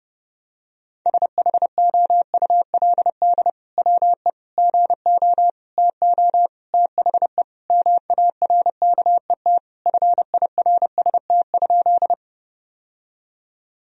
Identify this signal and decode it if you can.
Morse code, heard as a keyed tone, about 32 wpm, 700 Hz. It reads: SHOULD WE GO TO THE MARKET FIRST?
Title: Sentences from Top 1000 Words - Review Part 3 - 30wpm